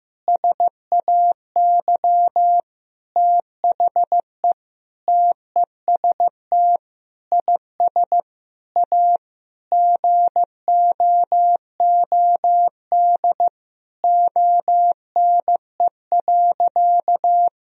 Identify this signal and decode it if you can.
Morse code, heard as a keyed tone, about 15 wpm, 700 Hz. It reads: SAY THE TEST IS A GOOD ONE.
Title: Sentences from Top 1000 Words - Review Part 3 - 15wpm